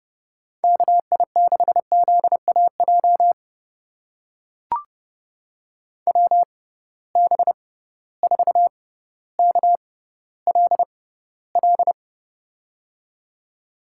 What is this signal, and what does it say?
Morse code, heard as a keyed tone, about 31 wpm, spaced as for 16 wpm, 700 Hz. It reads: KI6ZAJ WB4KLL
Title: USA Call Signs - 6 Characters - 6x Character Spacing - 30wpm